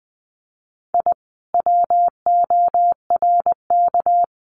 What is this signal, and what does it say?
Morse code, heard as a keyed tone, about 19 wpm, 700 Hz. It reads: I WORK
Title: Common QSO Phrases Encore 20wpm